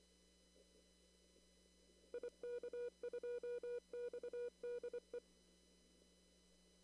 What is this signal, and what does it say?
Morse code, heard as a keyed tone, about 28 wpm, 480 Hz. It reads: IK2XDE
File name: CW / kt5j wpx cw-op-k5pi